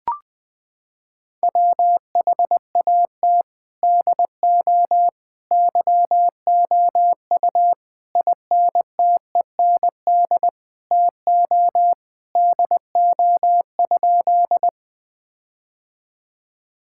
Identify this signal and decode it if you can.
Morse code, heard as a keyed tone, about 20 wpm, 700 Hz. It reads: WHAT DO YOU INTEND TO DO?
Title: Sentences from Top 1600 Words 20wpm